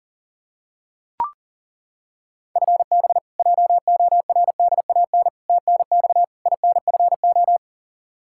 Morse code, heard as a keyed tone, about 37 wpm, 700 Hz. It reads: FB JORDAN TNX INFO
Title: Practice QSO - Exchange 3 - More Info - 40wpm